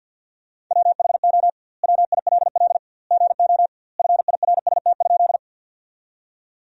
Morse code, heard as a keyed tone, about 50 wpm, 700 Hz. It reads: WHO WILL GO FIRST?
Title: 2x Word-Spacing - Sentences from Top 100 Words - 50wpm